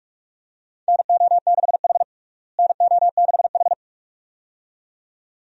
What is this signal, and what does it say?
Morse code, heard as a keyed tone, about 44 wpm, 700 Hz. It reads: NO6H NO6H
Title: USA Call Signs - 4 Characters - Repeated 2x - 45wpm